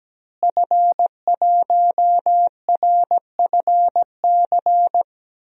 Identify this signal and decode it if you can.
Morse code, heard as a keyed tone, about 17 wpm, 700 Hz. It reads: F1RFC
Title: DXCC-100 Call Signs - 5 Characters - Repeated 3x - 17wpm